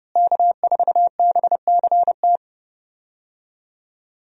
Morse code, heard as a keyed tone, about 32 wpm, 700 Hz. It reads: K4BCT
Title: USA Call Signs - 5 Characters - Repeated 3x - 30wpm